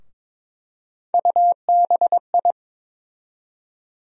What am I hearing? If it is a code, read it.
Morse code, heard as a keyed tone, about 22 wpm, 700 Hz. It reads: UBI
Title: USA Call Sign Postfixes - 3 Letters 22wpm